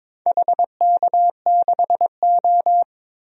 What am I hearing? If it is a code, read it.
Morse code, heard as a keyed tone, about 22 wpm, 700 Hz. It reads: HK6O
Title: DXCC-100 Call Signs - 4 Characters - 22wpm